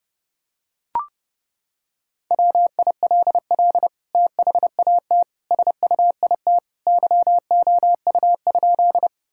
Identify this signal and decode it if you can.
Morse code, heard as a keyed tone, about 30 wpm, 700 Hz. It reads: WILL THAT SUIT YOU?
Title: Sentences from Top 1600 Words 30wpm